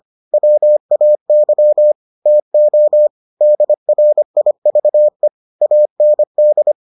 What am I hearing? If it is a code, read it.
Morse code, heard as a keyed tone, about 25 wpm, 600 Hz. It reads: WAY TO DRIVE AND